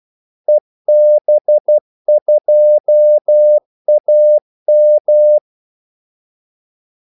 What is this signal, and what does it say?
Morse code, heard as a keyed tone, about 12 wpm, 600 Hz. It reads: EB2AM
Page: 2 ekstra ordmellemrum mellem kaldesignalerne